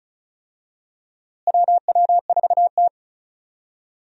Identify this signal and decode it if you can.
Morse code, heard as a keyed tone, about 34 wpm, 700 Hz. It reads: WW4T
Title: USA Call Signs - 4 Characters - Repeated 3x - 35wpm